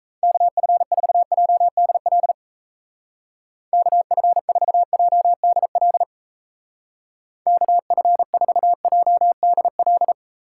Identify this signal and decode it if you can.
Morse code, heard as a keyed tone, about 35 wpm, 700 Hz. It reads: KF4JDL KF4JDL KF4JDL